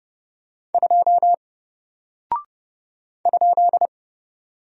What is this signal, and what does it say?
Morse code, heard as a keyed tone, about 30 wpm, 700 Hz. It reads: 2 ?
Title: Single Character (Letter:Number:Punctuation:Prosign) Alt 30wpm